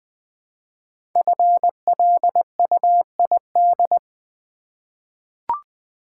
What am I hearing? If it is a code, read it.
Morse code, heard as a keyed tone, about 20 wpm, 700 Hz. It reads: FLUID
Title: 5 Letter Words - First 3 Letters Given - 20wpm